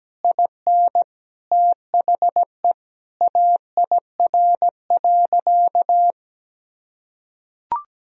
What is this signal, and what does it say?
Morse code, heard as a keyed tone, about 17 wpm, 700 Hz. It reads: IN THE AIR.
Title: Sentences from Top 1000 Words - Review Part 1 - 17wpm